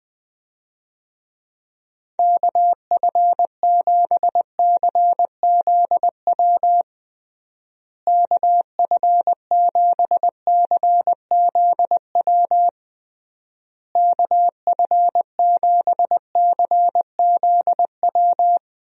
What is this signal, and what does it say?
Morse code, heard as a keyed tone, about 20 wpm, 700 Hz. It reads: KF7CZW KF7CZW KF7CZW
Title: USA Call Signs - 6 Characters - Repeated 3x - 20wpm